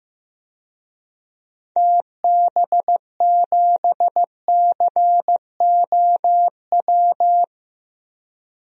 Morse code, heard as a keyed tone, about 15 wpm, 700 Hz. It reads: TB7COW
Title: DXCC-100 Call Signs - 6 Characters - Repeated 2x - 15wpm